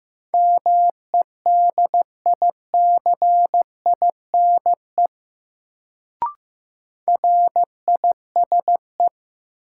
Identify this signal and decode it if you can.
Morse code, heard as a keyed tone, about 15 wpm, 700 Hz. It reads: MEDICINE RISE
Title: Sentences from Top 900 Words - New Word Drill - 15wpm